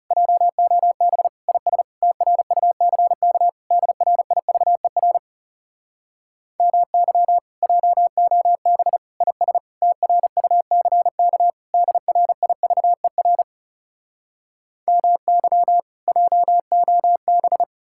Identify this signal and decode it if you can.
Morse code, heard as a keyed tone, about 40 wpm, 700 Hz. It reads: JOB IS TRUCK DRIVER MY JOB IS TRUCK DRIVER MY JOB